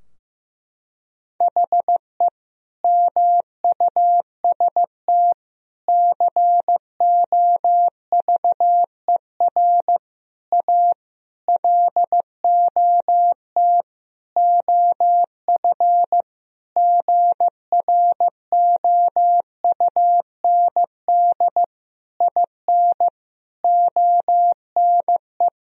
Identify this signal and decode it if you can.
Morse code, heard as a keyed tone, about 15 wpm, 700 Hz. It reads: HE MUST COVER A LOT OF GROUND IN ONE
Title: Sentences from Top 1000 Words - Review Part 3 - 15wpm